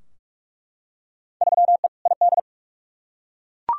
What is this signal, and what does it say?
Morse code, heard as a keyed tone, about 46 wpm, 700 Hz. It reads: 3E ID
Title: Field Day Exchanges Encore - 45wpm